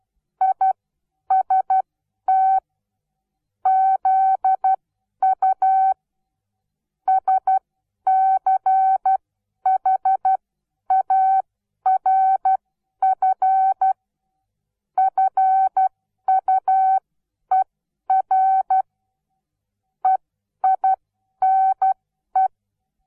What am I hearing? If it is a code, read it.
Morse code, heard as a keyed tone, about 12 wpm, spaced as for 10 wpm, 760 Hz. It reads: IST ZU SCHARF FUER EINE
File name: Onlinelehrgang / morsen / DJ4UF Morsen17